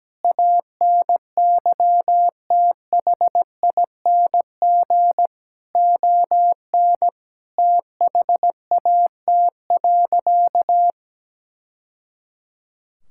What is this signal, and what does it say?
Morse code, heard as a keyed tone, about 17 wpm, 700 Hz. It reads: ANYTHING ON THAT.
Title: Sentences from Top 1000 Words - Review Part 5 - 17wpm